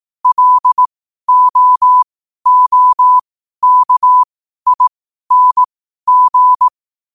Morse code, heard as a keyed tone, about 18 wpm, spaced as for 12 wpm, 1000 Hz. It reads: LOOKING
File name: qotdfiles / 12wpm